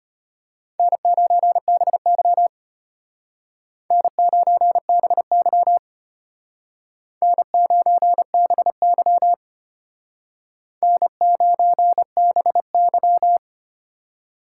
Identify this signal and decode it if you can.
Morse code, heard as a keyed tone, about 32 wpm, 700 Hz. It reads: N9BY N9BY N9BY N9BY